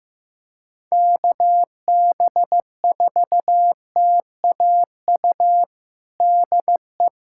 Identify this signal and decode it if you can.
Morse code, heard as a keyed tone, about 15 wpm, 700 Hz. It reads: KB4TAU DE
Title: USA Callsign DE Callsign - 6 Characters - 15wpm